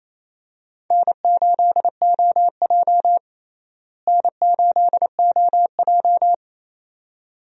Morse code, heard as a keyed tone, about 28 wpm, 700 Hz. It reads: N8OJ N8OJ
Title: USA Call Signs - 4 Characters - Repeated 2x - 28wpm